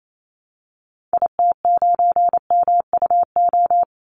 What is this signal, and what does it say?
Morse code, heard as a keyed tone, about 28 wpm, 700 Hz. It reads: IT9MUO
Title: Call Signs Repeated 2x - 28wpm